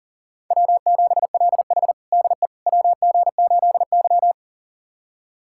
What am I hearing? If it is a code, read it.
Morse code, heard as a keyed tone, about 37 wpm, 700 Hz. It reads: W7LH DE WG8Y
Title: USA Callsign DE Callsign - 4 Characters - 40wpm